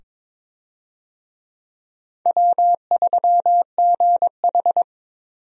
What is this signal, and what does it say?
Morse code, heard as a keyed tone, about 22 wpm, 700 Hz. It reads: W3GH
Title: USA Call Signs - 4 Characters - 22wpm